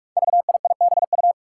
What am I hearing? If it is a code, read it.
Morse code, heard as a keyed tone, about 46 wpm, 700 Hz. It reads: VIIBU
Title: Five Challenging Letter-Number Combinations 45wpm